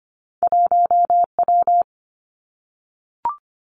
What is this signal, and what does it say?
Morse code, heard as a keyed tone, about 25 wpm, 700 Hz. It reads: 1W
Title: Two Challenging Letter-Number Combinations 25wpm